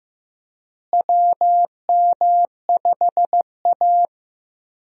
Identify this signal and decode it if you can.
Morse code, heard as a keyed tone, about 15 wpm, 700 Hz. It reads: WM5A
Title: USA Call Signs - 4 Characters - Repeated 3x - 15wpm